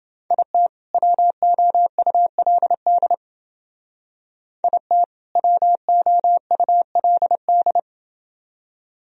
Morse code, heard as a keyed tone, about 28 wpm, 700 Hz. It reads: IT WOULD IT WOULD